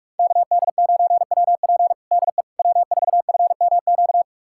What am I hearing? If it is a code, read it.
Morse code, heard as a keyed tone, about 41 wpm, 700 Hz. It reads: KD9WP DE W4FMQ
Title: USA Callsign DE Callsign - 5 Characters - 45wpm